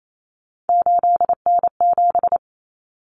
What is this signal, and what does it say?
Morse code, heard as a keyed tone, about 28 wpm, 700 Hz. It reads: 8N7
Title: Common Cut Numbers - 3-Digit 28wpm